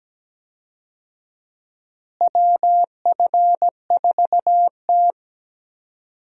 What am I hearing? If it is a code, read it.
Morse code, heard as a keyed tone, about 17 wpm, 700 Hz. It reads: WF4T
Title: USA Call Signs - 4 Characters - Repeated 3x - 17wpm